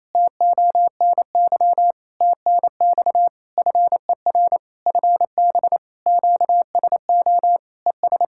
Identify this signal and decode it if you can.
Morse code, heard as a keyed tone, about 28 wpm, 700 Hz. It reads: TONY TNX FER FB QSO ES
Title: Practice QSO - Exchange 4 - Ending - 28wpm